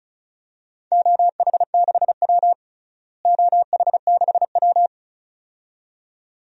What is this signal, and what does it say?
Morse code, heard as a keyed tone, about 36 wpm, 700 Hz. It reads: OH6W OH6W
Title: DXCC-100 Call Signs - 4 Characters - Repeated 2x - 35wpm